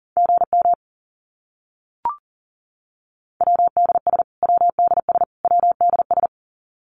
Morse code, heard as a keyed tone, about 40 wpm, 700 Hz. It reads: GM WDS WDS WDS
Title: Abbreviations Repeated 3x - 40wpm